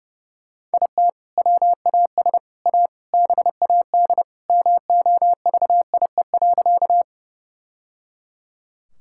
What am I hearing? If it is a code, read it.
Morse code, heard as a keyed tone, about 30 wpm, 700 Hz. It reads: IT WAS A BAD MOVIE.
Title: Sentences from Top 1000 Words - Review Part 5 - 30wpm